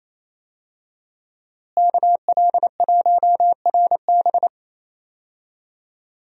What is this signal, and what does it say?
Morse code, heard as a keyed tone, about 27 wpm, 700 Hz. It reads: KL1RB
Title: USA Call Signs - 5 Characters - Repeated 3x - 28wpm